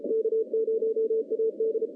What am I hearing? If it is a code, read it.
Morse code, heard as a keyed tone, about 34 wpm, 410 Hz. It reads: K0AD
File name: contests / CW-SS-k5tr- -0200